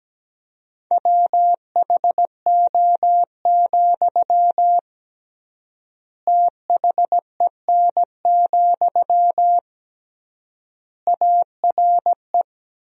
Morse code, heard as a keyed tone, about 17 wpm, 700 Hz. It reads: WHO, THEN, ARE
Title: 3x Word-Spacing - Sentences from Top 100 Words - 17wpm